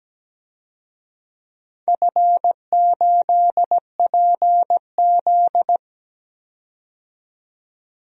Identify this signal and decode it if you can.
Morse code, heard as a keyed tone, about 17 wpm, 700 Hz. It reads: F8PZ
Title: DXCC-100 Call Signs - 4 Characters - 17wpm